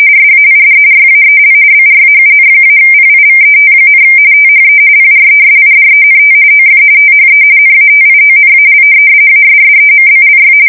RTTY, about 45.45 baud. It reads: SAID YAEL BRAUN-PIVET, THE PRESIDENT OF THE NATIONAL ASSEMBLY
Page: 170 Shift 2125/2295